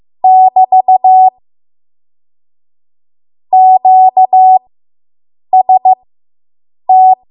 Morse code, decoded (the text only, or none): <BT> QST